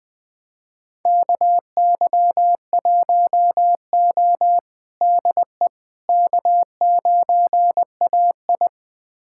Morse code, heard as a keyed tone, about 20 wpm, 700 Hz. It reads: KY1O DE K9AI
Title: USA Callsign DE Callsign - 4 Characters - 20wpm